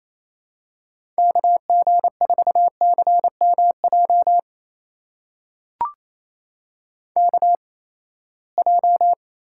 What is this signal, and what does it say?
Morse code, heard as a keyed tone, about 28 wpm, spaced as for 19 wpm, 700 Hz. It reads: KG4CMJ KJ
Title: USA Call Signs - 6 Characters - 8x Character Spacing - 28wpm